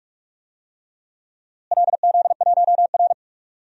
Morse code, heard as a keyed tone, about 46 wpm, 700 Hz. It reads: LZ1R
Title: DXCC-100 Call Signs - 4 Characters - Repeated 3x - 45wpm